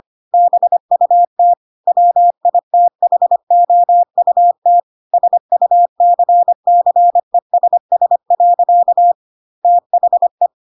Morse code, heard as a keyed tone, about 25 wpm, 700 Hz. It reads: BUT WITHOUT SUCCESS. THE
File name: Pinocchio / Pinocchio ch3 25 25 ik1hge